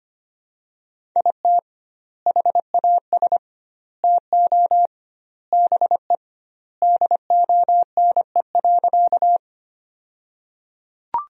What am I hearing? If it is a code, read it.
Morse code, heard as a keyed tone, about 25 wpm, 700 Hz. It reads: IT HAS TO BE DONE.
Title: 2x Word-Spacing - Sentences from Top 500 Words - 25wpm